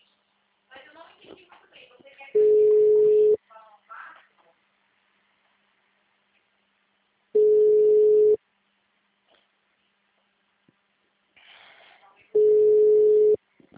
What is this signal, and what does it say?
Ringback tone (Europe, 425 Hz), with speech.
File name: out- -2014- - - .105